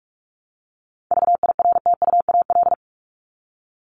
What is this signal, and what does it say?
Morse code, heard as a keyed tone, about 44 wpm, 700 Hz. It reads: VIRTUAL
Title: LY Suffix - Top 100 Words with Base Words 45wpm